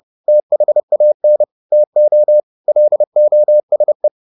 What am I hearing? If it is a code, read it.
Morse code, heard as a keyed tone, about 30 wpm, 600 Hz. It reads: THAN TO LOSE